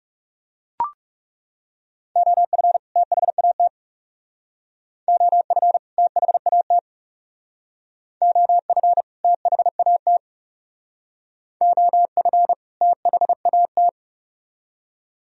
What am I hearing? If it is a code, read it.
Morse code, heard as a keyed tone, about 36 wpm, 700 Hz. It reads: OF THAT OF THAT OF THAT OF THAT
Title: Speed-Racing - Sets of 2 Words 30wpm